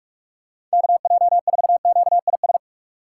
Morse code, heard as a keyed tone, about 46 wpm, 690 Hz. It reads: KJ4QIS
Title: USA Call Signs - 6 Characters - 2x Character Spacing - 45wpm